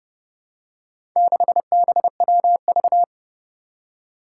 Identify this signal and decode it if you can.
Morse code, heard as a keyed tone, about 30 wpm, 700 Hz. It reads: 6BWV
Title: Four Challenging Letter-Number Combinations 30wpm